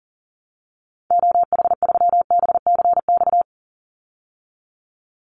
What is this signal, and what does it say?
Morse code, heard as a keyed tone, about 40 wpm, 700 Hz. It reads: OH3BCX
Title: Call Signs Repeated 2x - 40wpm